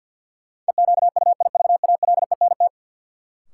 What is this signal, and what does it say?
Morse code, heard as a keyed tone, about 49 wpm, 700 Hz. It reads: EQUIVALENT
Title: Sentences from Top 1600 Words - New Word Drill - 50wpm